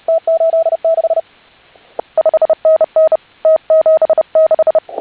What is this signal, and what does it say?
Morse code, heard as a keyed tone, about 32 wpm, 660 Hz. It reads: T86 5NN T76
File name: HA6A CW